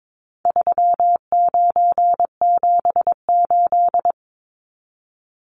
Morse code, heard as a keyed tone, about 22 wpm, 700 Hz. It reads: 3978
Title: Numbers - 4-Digit 22wpm